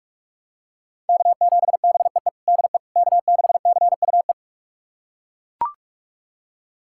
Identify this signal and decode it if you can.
Morse code, heard as a keyed tone, about 41 wpm, 700 Hz. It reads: K7BEE DE K6CUE
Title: USA Callsign DE Callsign - 5 Characters - 45wpm